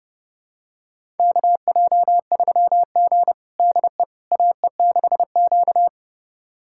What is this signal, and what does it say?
Morse code, heard as a keyed tone, about 30 wpm, 700 Hz. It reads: KJ3G DE AE6Q
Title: USA Callsign DE Callsign - 4 Characters - 30wpm